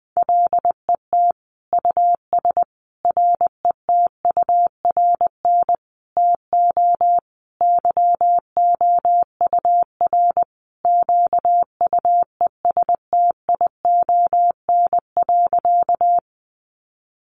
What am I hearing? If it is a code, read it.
Morse code, heard as a keyed tone, about 20 wpm, 700 Hz. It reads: LET US RETURN TO YOUR QUESTION.
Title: Sentences from Top 800 Words 20wpm